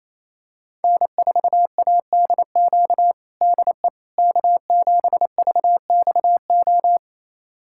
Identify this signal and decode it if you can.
Morse code, heard as a keyed tone, about 27 wpm, 700 Hz. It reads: N4ADQ DE K7VXO
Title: USA Callsign DE Callsign - 5 Characters - 28wpm